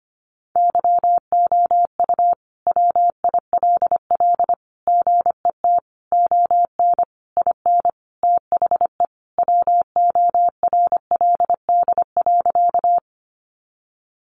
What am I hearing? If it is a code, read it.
Morse code, heard as a keyed tone, about 25 wpm, 700 Hz. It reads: YOU WILL GET ON IN THE WORLD.
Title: Sentences from Top 200 Words 25wpm